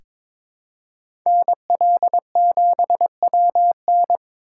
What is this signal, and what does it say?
Morse code, heard as a keyed tone, about 22 wpm, 700 Hz. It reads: NL7WN
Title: USA Call Signs - 5 Characters - 2x Character Spacing - 22wpm